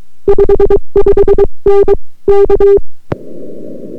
Morse code, heard as a keyed tone, about 22 wpm, 400 Hz. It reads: 55N K